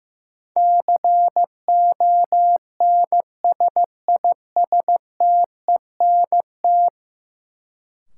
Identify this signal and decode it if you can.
Morse code, heard as a keyed tone, about 15 wpm, 700 Hz. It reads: CONSISTENT
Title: Sentences from Top 1500 Words - New Word Drill Alt - 15wpm